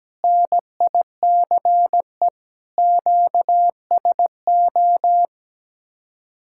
Morse code, heard as a keyed tone, about 17 wpm, 700 Hz. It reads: NICE QSO
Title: Speed-Racing Common QSO Phrases - 17wpm